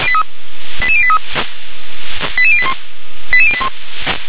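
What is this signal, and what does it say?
Five-tone selective call (ZVEI), heard as quick runs of five tones, tones of about 70 ms. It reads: 88083 88081 88081
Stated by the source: ZVEI five-tone selective call received on an SDR.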